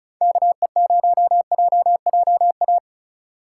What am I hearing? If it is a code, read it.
Morse code, heard as a keyed tone, about 36 wpm, 700 Hz. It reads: KE0JJA
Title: USA Call Signs - 6 Characters - 3x Character Spacing - 35wpm